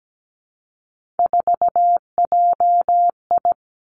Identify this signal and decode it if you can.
Morse code, heard as a keyed tone, about 17 wpm, 700 Hz. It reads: 4JI
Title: Three Challenging Letter-Number Combinations 17wpm